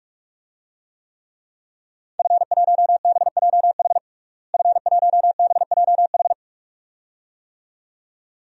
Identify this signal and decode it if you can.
Morse code, heard as a keyed tone, about 41 wpm, 700 Hz. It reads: F1BJH F1BJH
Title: DXCC-100 Call Signs - 5 Characters - Repeated 2x - 45wpm